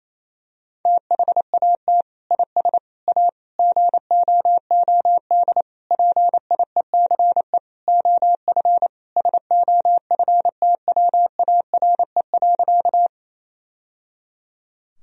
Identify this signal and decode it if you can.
Morse code, heard as a keyed tone, about 28 wpm, 700 Hz. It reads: THAT IS A GOOD PIECE OF SOFTWARE.
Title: Sentences from Top 1000 Words - Review Part 3 - 28wpm